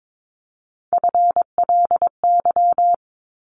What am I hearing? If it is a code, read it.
Morse code, heard as a keyed tone, about 22 wpm, 700 Hz. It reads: FLY
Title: Top 500 Words 22wpm